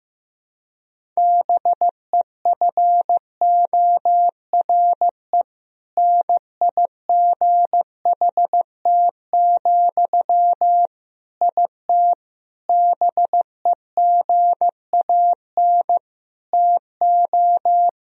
Morse code, heard as a keyed tone, about 15 wpm, 700 Hz. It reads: BEFORE NIGHT, IT BEGAN TO
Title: Sentences from Top 1000 Words - Review Part 1 - 15wpm